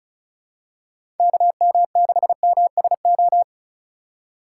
Morse code, heard as a keyed tone, about 36 wpm, 700 Hz. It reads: KM6MSO
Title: USA Call Signs - 2x Character Spacing - 35wpm